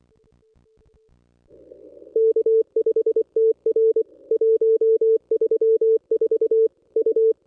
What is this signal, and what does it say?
Morse code, heard as a keyed tone, about 24 wpm, 450 Hz. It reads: K5TR 134 U
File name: contests / ss-cw-k5tr-op-n5rz -1200